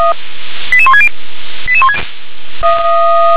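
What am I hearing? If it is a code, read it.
Five-tone selective call (ZVEI), heard as quick runs of five tones, tones of about 70 ms. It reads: 88160
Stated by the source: ZVEI five-tone call received on SDR.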